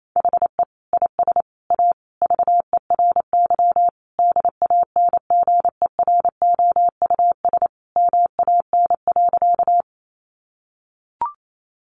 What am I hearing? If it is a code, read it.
Morse code, heard as a keyed tone, about 28 wpm, 700 Hz. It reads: HE IS A VERY DANGEROUS MAN.
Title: Sentences from Top 1200 Words 28wpm